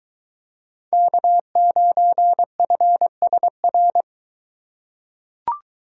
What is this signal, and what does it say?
Morse code, heard as a keyed tone, about 22 wpm, 700 Hz. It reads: K9FSR